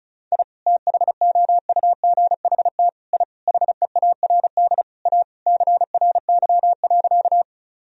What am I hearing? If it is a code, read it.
Morse code, heard as a keyed tone, about 34 wpm, 700 Hz. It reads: I THOUGHT I HEARD A CRY.
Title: Sentences from Top 1000 Words - Review Part 2 - 35wpm